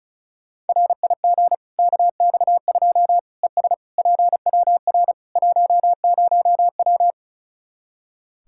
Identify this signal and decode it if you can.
Morse code, heard as a keyed tone, about 34 wpm, 700 Hz. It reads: RIG KX2 ES PWR 10W
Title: K1USN SST QSO Practice 35wpm